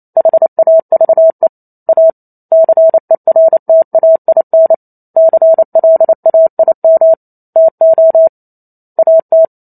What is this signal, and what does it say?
Morse code, heard as a keyed tone, about 28 wpm, 650 Hz. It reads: HAVE A CERTAIN CLAIM TO AT
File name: Chapter 2 - 28 WPM